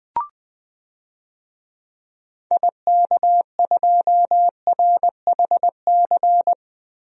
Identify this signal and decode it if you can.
Morse code, heard as a keyed tone, about 21 wpm, 690 Hz. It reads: IK2RHC
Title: DXCC-100 Call Signs - 6 Characters - Repeated 2x - 20wpm